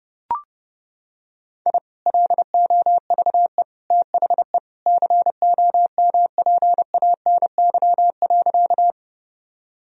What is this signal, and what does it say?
Morse code, heard as a keyed tone, about 32 wpm, 700 Hz. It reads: I LOVE THE COMPANY.
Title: Sentences from Top 1000 Words - Review Part 3 - 30wpm